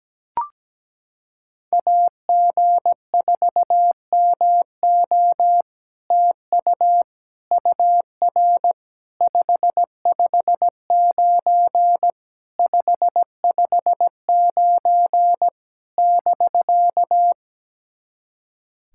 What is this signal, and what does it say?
Morse code, heard as a keyed tone, about 17 wpm, 700 Hz. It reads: AG4MO TU UR 559 559 <BK>
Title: SOTA QSO Practice 17wpm